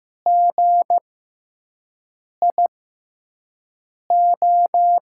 Morse code, heard as a keyed tone, about 15 wpm, spaced as for 5 wpm, 700 Hz. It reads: GIO
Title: USA Call Signs - 6x Character Spacing - 15wpm